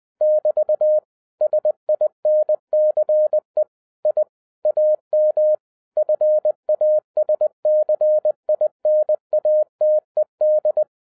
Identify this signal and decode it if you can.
Morse code, heard as a keyed tone, about 20 wpm, 610 Hz. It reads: <BT> SINCE I AM FASCINATED